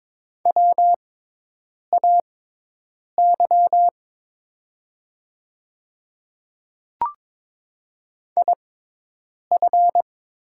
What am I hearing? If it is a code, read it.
Morse code, heard as a keyed tone, about 22 wpm, spaced as for 8 wpm, 700 Hz. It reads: WAY IF
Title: 6x Character Spacing - Top 300 Words - 22wpm